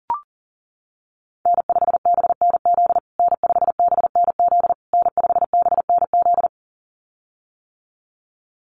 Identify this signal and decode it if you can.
Morse code, heard as a keyed tone, about 40 wpm, 700 Hz. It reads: N5BNZ N5BNZ N5BNZ
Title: USA Call Signs Repeated 3x 40wpm